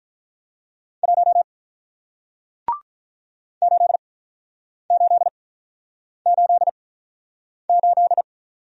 Morse code, heard as a keyed tone, about 47 wpm, spaced as for 9 wpm, 700 Hz. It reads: 1 8888